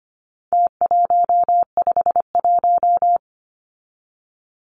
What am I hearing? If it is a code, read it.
Morse code, heard as a keyed tone, about 25 wpm, 700 Hz. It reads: T151
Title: Numbers - Contest Style - 4-Digit 25wpm